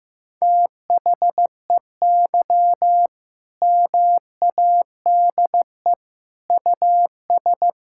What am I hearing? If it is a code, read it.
Morse code, heard as a keyed tone, about 15 wpm, 700 Hz. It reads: THEY MADE US
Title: Sentences from Top 1000 Words - Review Part 6 - 15wpm